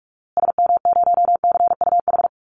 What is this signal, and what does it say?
Morse code, heard as a keyed tone, about 44 wpm, 700 Hz. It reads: SM0CUH
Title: Call Signs Repeated 2x - 45wpm